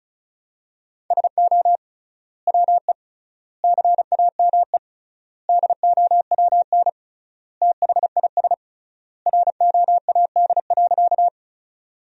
Morse code, heard as a keyed tone, about 34 wpm, 700 Hz. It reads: SO WE CAME DOWN THIS ROAD.
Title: 3x Word-Spacing - Sentences from Top 400 Words - 35wpm